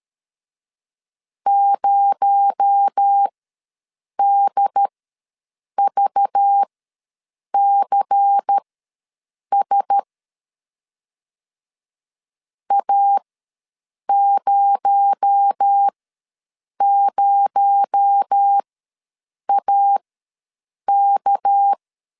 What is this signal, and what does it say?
Morse code, heard as a keyed tone, about 13 wpm, spaced as for 7 wpm, 800 Hz. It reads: 0DVCS A00AK
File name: LEZIONI / 6 PKX0 / 6 PKX0 ws 12wpm 3448